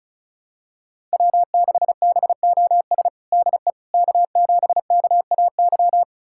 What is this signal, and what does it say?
Morse code, heard as a keyed tone, about 34 wpm, 700 Hz. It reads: W6BOS DE K7KAY
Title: USA Callsign DE Callsign - 5 Characters - 35wpm